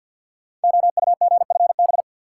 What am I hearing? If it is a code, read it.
Morse code, heard as a keyed tone, about 49 wpm, 700 Hz. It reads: OUGFB
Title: Code Groups - 5 Letters - 50wpm